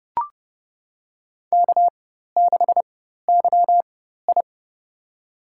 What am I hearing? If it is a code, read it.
Morse code, heard as a keyed tone, about 30 wpm, spaced as for 14 wpm, 700 Hz. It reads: K6YI
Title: USA Call Signs - 4x Character Spacing - 30wpm